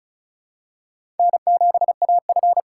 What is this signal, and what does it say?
Morse code, heard as a keyed tone, about 34 wpm, 700 Hz. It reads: N7AF
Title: USA Call Signs - 4 Characters - 6x Character Spacing - 35wpm